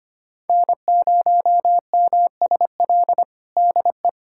Morse code, heard as a keyed tone, about 25 wpm, 700 Hz. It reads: N0MSL DE
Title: USA Callsign DE Callsign - 5 Characters - 25wpm